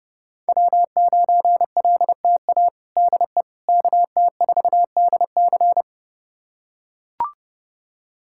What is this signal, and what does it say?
Morse code, heard as a keyed tone, about 30 wpm, 700 Hz. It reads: W9LTA DE KT4DC
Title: USA Callsign DE Callsign - 5 Characters - 30wpm